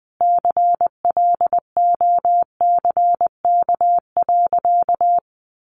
Morse code, heard as a keyed tone, about 20 wpm, 700 Hz. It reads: CLOCK.
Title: Easy Phrases 20wpm